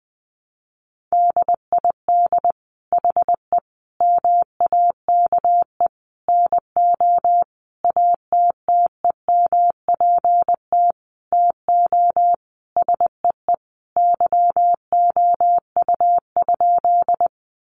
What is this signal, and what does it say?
Morse code, heard as a keyed tone, about 19 wpm, 700 Hz. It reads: DID HE MAKE NO ATTEMPT TO SEE YOU?
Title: Sentences from Top 1000 Words 20wpm